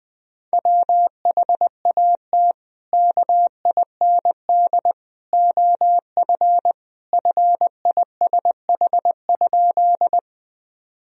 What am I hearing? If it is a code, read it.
Morse code, heard as a keyed tone, about 20 wpm, 700 Hz. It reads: WHAT KIND OF FISH?
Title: Sentences from Top 1000 Words - Review Part 1 - 20wpm